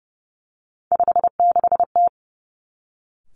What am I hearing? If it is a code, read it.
Morse code, heard as a keyed tone, about 30 wpm, 700 Hz. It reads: 56T
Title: Three Challenging Letter-Number Combinations 30wpm